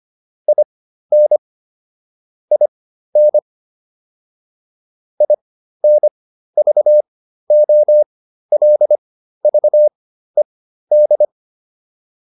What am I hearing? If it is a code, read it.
Morse code, heard as a keyed tone, about 25 wpm, spaced as for 13 wpm, 600 Hz. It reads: IN IN INVOLVED